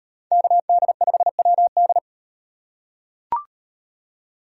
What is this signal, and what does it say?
Morse code, heard as a keyed tone, about 38 wpm, 700 Hz. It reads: KD5WD